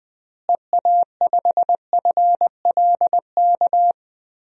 Morse code, heard as a keyed tone, about 20 wpm, 700 Hz. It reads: EA5FLK
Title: DXCC-100 Call Signs - 6 Characters - Repeated 3x - 20wpm